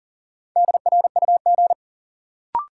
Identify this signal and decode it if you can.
Morse code, heard as a keyed tone, about 40 wpm, 700 Hz. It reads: DRUG
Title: Sets of 5 Words - New Word Drill - 40wpm